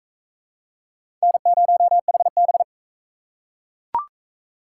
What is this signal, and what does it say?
Morse code, heard as a keyed tone, about 41 wpm, 700 Hz. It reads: N0HB